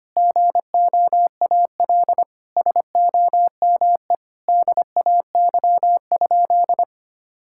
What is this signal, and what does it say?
Morse code, heard as a keyed tone, about 25 wpm, 700 Hz. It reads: GOAL SOME DAY?
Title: Sentences from Top 1000 Words - Review Part 6 - 25wpm